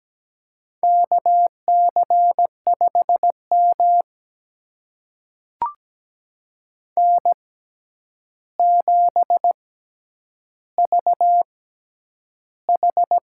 Morse code, heard as a keyed tone, about 17 wpm, spaced as for 9 wpm, 700 Hz. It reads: KC5M N7VH
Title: USA Call Signs - 4 Characters - 6x Character Spacing - 17wpm